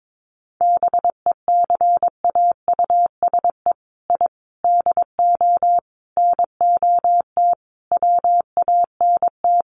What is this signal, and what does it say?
Morse code, heard as a keyed tone, about 22 wpm, 700 Hz. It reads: BECAUSE I DO NOT WANT
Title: Sets of 5 Words Encore 22wpm